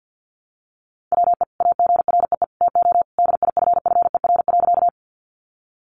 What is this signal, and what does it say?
Morse code, heard as a keyed tone, about 49 wpm, 700 Hz. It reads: WE AGREE TO DIFFER.
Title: Sentences from Top 1100 Words 50wpm